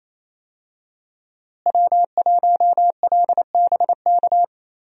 Morse code, heard as a keyed tone, about 27 wpm, 700 Hz. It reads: W1LBK
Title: USA Call Signs - 5 Characters - Repeated 3x - 28wpm